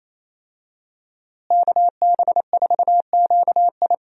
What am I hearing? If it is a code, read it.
Morse code, heard as a keyed tone, about 27 wpm, 700 Hz. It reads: KB4QI
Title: USA Call Signs - 5 Characters - Repeated 3x - 28wpm